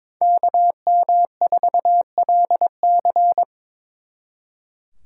Morse code, heard as a keyed tone, about 26 wpm, 700 Hz. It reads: KM4LC
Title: Speed Racing - USA 2x2 Call Signs 22wpm